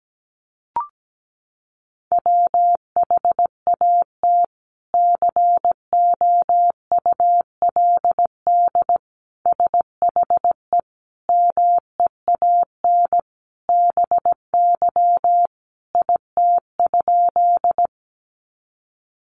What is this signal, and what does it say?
Morse code, heard as a keyed tone, about 17 wpm, 700 Hz. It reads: WHAT COULD SHE MEAN BY IT?
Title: Sentences from Top 200 Words 17wpm